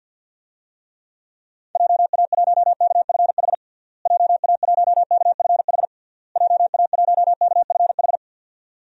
Morse code, heard as a keyed tone, about 52 wpm, 700 Hz. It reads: JA1KFH JA1KFH JA1KFH
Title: DXCC-100 Call Signs - 6 Characters - Repeated 3x - 50wpm